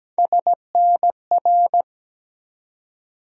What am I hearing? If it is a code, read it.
Morse code, heard as a keyed tone, about 17 wpm, 700 Hz. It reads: SNR
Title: USA Call Sign Postfixes - 3 Letters 17wpm